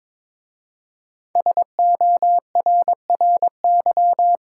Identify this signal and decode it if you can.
Morse code, heard as a keyed tone, about 22 wpm, 700 Hz. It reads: SORRY
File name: 5 Letter Words - First 3 Letters Given - 22wpm